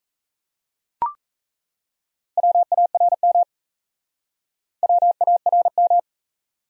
Morse code, heard as a keyed tone, about 38 wpm, 700 Hz. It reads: WARM WARM